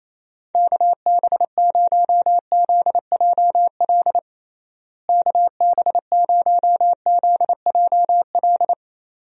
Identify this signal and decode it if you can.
Morse code, heard as a keyed tone, about 28 wpm, 700 Hz. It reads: KB0ZJL KB0ZJL
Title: USA Call Signs - 6 Characters - Repeated 2x - 28wpm